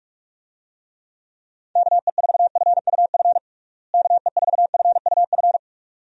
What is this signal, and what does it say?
Morse code, heard as a keyed tone, about 46 wpm, 700 Hz. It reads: KE4FUF KE4FUF
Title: USA Call Signs - 6 Characters - Repeated 2x - 45wpm